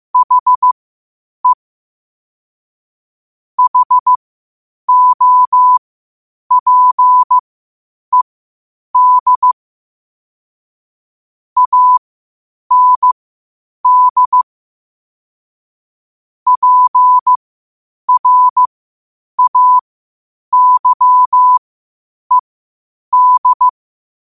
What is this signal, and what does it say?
Morse code, heard as a keyed tone, about 15 wpm, spaced as for 8 wpm, 1000 Hz. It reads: HE HOPED AND PRAYED
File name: qotdfiles / 8wpm